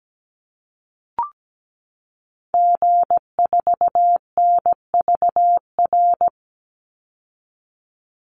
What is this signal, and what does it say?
Morse code, heard as a keyed tone, about 17 wpm, 700 Hz. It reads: G4NVR
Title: Call Signs Repeated 3x - 17wpm